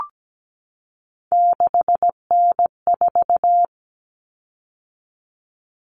Morse code, heard as a keyed tone, about 17 wpm, 700 Hz. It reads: 6N4
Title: Common Cut Numbers - 3-Digit 17wpm